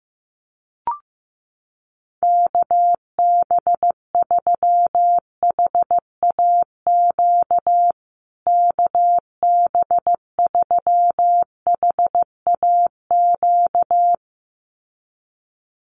Morse code, heard as a keyed tone, about 15 wpm, 700 Hz. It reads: KB3HAQ KB3HAQ
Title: USA Call Signs Repeated 2x 15wpm